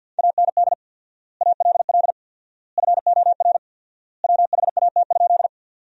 Morse code, heard as a keyed tone, about 53 wpm, 700 Hz. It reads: AND ALL FOR WHAT?
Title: 4x Word-Spacing - Sentences from Top 100 Words - 50wpm